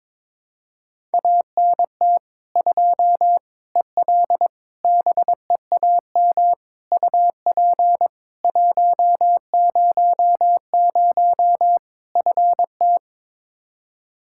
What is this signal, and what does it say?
Morse code, heard as a keyed tone, about 22 wpm, 700 Hz. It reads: ANT 2 EL BEAM UP 100 FT
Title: Practice QSO - Exchange 2 - Info - 22wpm